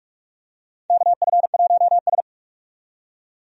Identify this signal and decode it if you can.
Morse code, heard as a keyed tone, about 48 wpm, 700 Hz. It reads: KF1S
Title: USA Call Signs - 4 Characters - 45wpm